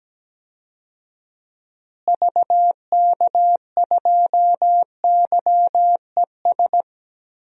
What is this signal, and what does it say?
Morse code, heard as a keyed tone, about 17 wpm, 700 Hz. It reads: VK2YES
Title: DXCC-100 Call Signs - 6 Characters - Repeated 2x - 17wpm